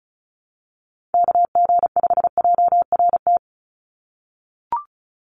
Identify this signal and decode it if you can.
Morse code, heard as a keyed tone, about 34 wpm, 700 Hz. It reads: KG5JRT
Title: USA Call Signs 35wpm